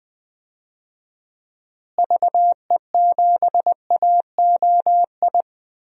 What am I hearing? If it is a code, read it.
Morse code, heard as a keyed tone, about 20 wpm, 700 Hz. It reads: VE7AOI
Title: DXCC-100 Call Signs - 6 Characters - Repeated 3x - 20wpm